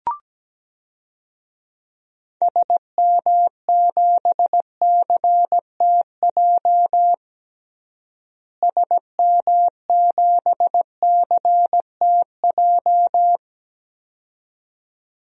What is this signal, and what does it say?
Morse code, heard as a keyed tone, about 17 wpm, 700 Hz. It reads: SM7CTJ SM7CTJ
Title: DXCC-100 Call Signs - 6 Characters - Repeated 2x - 17wpm